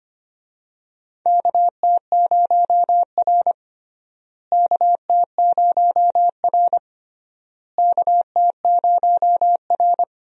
Morse code, heard as a keyed tone, about 25 wpm, 700 Hz. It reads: KT0R KT0R KT0R
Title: USA Call Signs - 4 Characters - Repeated 3x - 25wpm